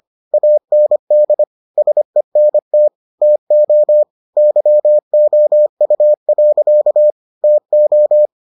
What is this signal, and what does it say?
Morse code, heard as a keyed tone, about 25 wpm, 600 Hz. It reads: AND SENT TO YOU. TO